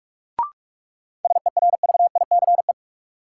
Morse code, heard as a keyed tone, about 46 wpm, 700 Hz. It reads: SERVICE
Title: Sentences from Top 700 Words - New Word Drill - 45wpm